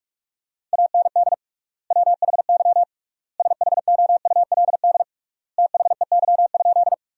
Morse code, heard as a keyed tone, about 46 wpm, 700 Hz. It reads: AND WHY SHOULD THEY?
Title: 3x Word-Spacing - Sentences from Top 300 Words - 45wpm